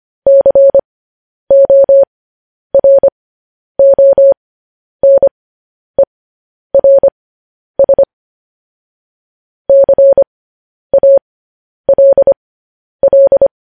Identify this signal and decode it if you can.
Morse code, heard as a keyed tone, about 25 wpm, spaced as for 10 wpm, 550 Hz. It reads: CORONERS CALL
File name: podcast / News headlines-1x-25x10